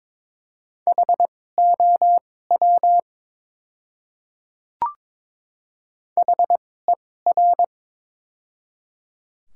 Morse code, heard as a keyed tone, about 22 wpm, spaced as for 16 wpm, 700 Hz. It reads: HOW HER
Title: 2x Character Spacing - Top 200 Words - 22wpm